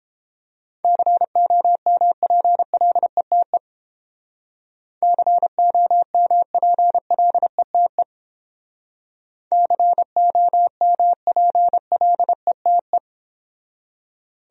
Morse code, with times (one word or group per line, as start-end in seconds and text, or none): 0.8-3.6 COMPLETE
5.0-8.0 COMPLETE
9.5-13.0 COMPLETE